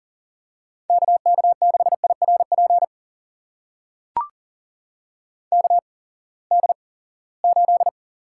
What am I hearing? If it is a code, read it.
Morse code, heard as a keyed tone, about 40 wpm, spaced as for 23 wpm, 700 Hz. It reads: KK6IRP KD8
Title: USA Call Signs - 6 Characters - 8x Character Spacing - 40wpm